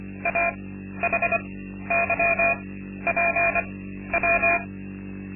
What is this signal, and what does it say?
Morse code, heard as a keyed tone, about 24 wpm, spaced as for 12 wpm, 680 Hz. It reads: AHYPW
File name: Linksmakalnis-2